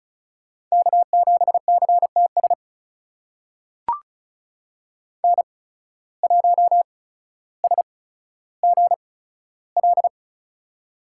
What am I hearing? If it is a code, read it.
Morse code, heard as a keyed tone, about 36 wpm, spaced as for 15 wpm, 700 Hz. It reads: K7CTS N1SGL
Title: USA Call Signs - 5 Characters - 8x Character Spacing - 35wpm